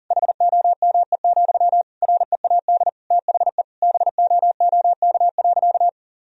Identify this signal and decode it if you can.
Morse code, heard as a keyed tone, about 40 wpm, 700 Hz. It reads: HOME, READ THE BOOK.
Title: Sentences from Top 1000 Words - Review Part 2 - 40wpm